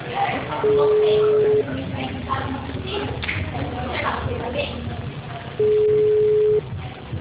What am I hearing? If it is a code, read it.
Ringback tone (Europe, 425 Hz), with speech.